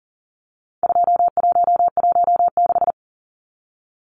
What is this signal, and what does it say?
Morse code, heard as a keyed tone, about 40 wpm, 700 Hz. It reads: 2116
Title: Numbers - 4-Digit 40wpm